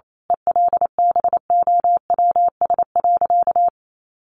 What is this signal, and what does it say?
Morse code, heard as a keyed tone, about 28 wpm, 700 Hz. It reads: ELBOWS.
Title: A Sound Of Thunder 28wpm